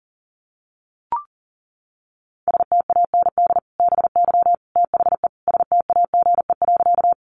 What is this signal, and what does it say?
Morse code, heard as a keyed tone, about 40 wpm, 700 Hz. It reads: STAND BY THE STAGE.
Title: Sentences from Top 1200 Words 40wpm